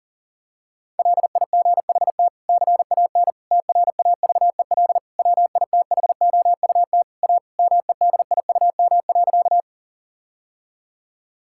Morse code, heard as a keyed tone, about 40 wpm, 700 Hz. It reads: LIGHT CAN TRAVEL WITHOUT A MEDIUM.
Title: Sentences from Top 1000 Words - Review Part 6 - 40wpm